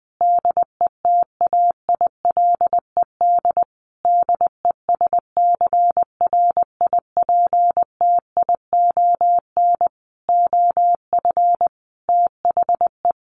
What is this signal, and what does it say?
Morse code, heard as a keyed tone, about 20 wpm, 700 Hz. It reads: DETAILED DESCRIPTION OF THE
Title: Sentences from Top 1500 Words 20wpm